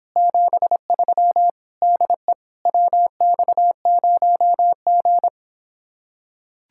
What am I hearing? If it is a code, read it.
Morse code, heard as a keyed tone, about 26 wpm, 700 Hz. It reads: 73 DE WX0G